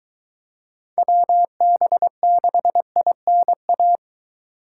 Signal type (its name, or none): Morse code